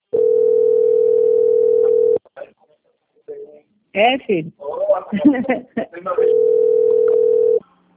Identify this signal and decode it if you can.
Ringback tone (North America), 440 + 480 Hz, with speech.
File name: out- - - - - .19377